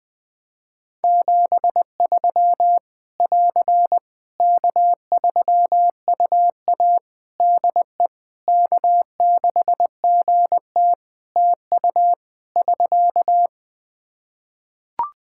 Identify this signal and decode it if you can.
Morse code, heard as a keyed tone, about 20 wpm, 700 Hz. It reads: 73 <AR> K3UA DE K6GT TU <SK>
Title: Practice QSO - Exchange 4 - Ending - 20wpm